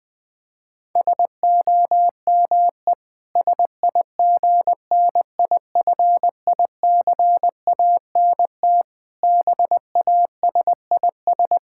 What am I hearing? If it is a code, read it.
Morse code, heard as a keyed tone, about 20 wpm, 700 Hz. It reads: SOME SIGNIFICANT BASIS
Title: Sentences from Top 1600 Words 20wpm